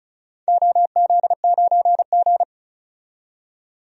Morse code, heard as a keyed tone, about 34 wpm, 700 Hz. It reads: OZ9G
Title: DXCC-100 Call Signs - 4 Characters - 35wpm